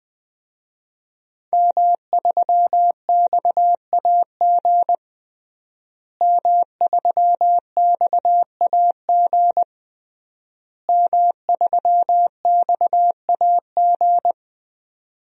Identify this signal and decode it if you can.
Morse code, heard as a keyed tone, about 20 wpm, 700 Hz. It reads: M3XAG M3XAG M3XAG
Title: DXCC-100 Call Signs - 5 Characters - Repeated 3x - 20wpm